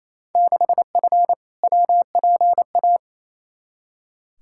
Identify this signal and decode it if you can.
Morse code, heard as a keyed tone, about 28 wpm, 700 Hz. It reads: 6F WPA
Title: Field Day Exchanges Encore - 28wpm